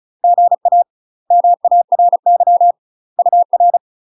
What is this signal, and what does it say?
Morse code, heard as a keyed tone, about 36 wpm, 700 Hz. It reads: GA MARY UR
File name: UK MASTER QSOs TRAINING / 35WPM / UK MASTER QSOs 0007